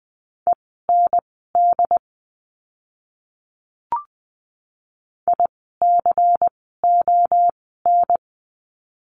Morse code, heard as a keyed tone, about 20 wpm, spaced as for 15 wpm, 700 Hz. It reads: END ICON
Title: 094 Lesson 16f - Letter c - Words 20wpm